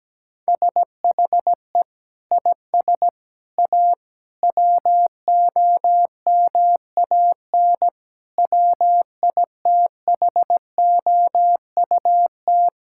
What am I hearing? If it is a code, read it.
Morse code, heard as a keyed tone, about 17 wpm, 700 Hz. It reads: SHE IS A WOMAN WITHOUT
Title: Sentences from Top 1600 Words 17wpm